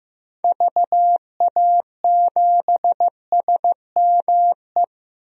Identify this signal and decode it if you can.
Morse code, heard as a keyed tone, about 15 wpm, 700 Hz. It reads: VA7SME
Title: DXCC-100 Call Signs - 6 Characters - 15wpm